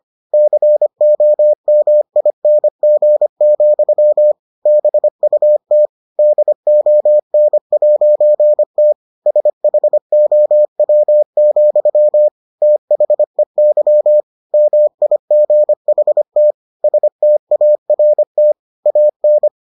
Morse code, heard as a keyed tone, about 25 wpm, 600 Hz. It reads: COMING, BUT DON'T SHOW, THEY MIGHT START AN